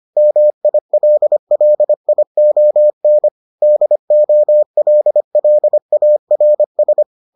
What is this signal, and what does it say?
Morse code, heard as a keyed tone, about 25 wpm, 600 Hz. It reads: MILLION DOLLARS